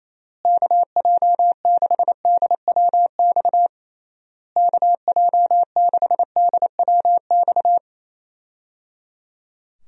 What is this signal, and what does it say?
Morse code, heard as a keyed tone, about 28 wpm, 700 Hz. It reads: KJ6DWX KJ6DWX
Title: USA Call Signs - 6 Characters - Repeated 3x - 28wpm